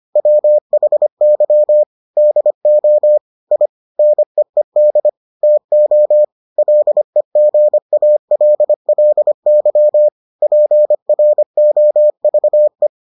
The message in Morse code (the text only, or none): WHY DO I NEED TO LEGALLY PROVE